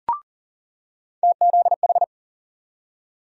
Morse code, heard as a keyed tone, about 40 wpm, 700 Hz. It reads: TZH
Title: USA Call Sign Postfixes - 3 Letters 40wpm